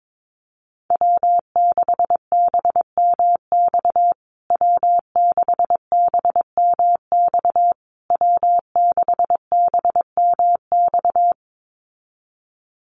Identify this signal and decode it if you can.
Morse code, heard as a keyed tone, about 22 wpm, 690 Hz. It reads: W6BMX W6BMX W6BMX
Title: USA Call Signs Repeated 3x 22wpm